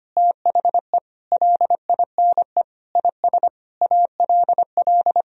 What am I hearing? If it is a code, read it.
Morse code, heard as a keyed tone, about 25 wpm, 700 Hz. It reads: THE LINE IS ALL
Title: Sentences from Top 1000 Words - Review Part 1 - 25wpm